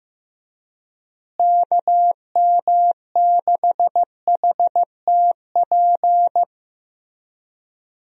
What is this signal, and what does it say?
Morse code, heard as a keyed tone, about 16 wpm, 700 Hz. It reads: KM6HTP
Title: USA Call Signs - 6 Characters - Repeated 2x - 15wpm